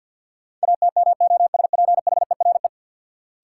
Morse code, heard as a keyed tone, about 50 wpm, 700 Hz. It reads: ATMOSPHERE
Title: Sentences from Top 1600 Words - New Word Drill Alt - 50wpm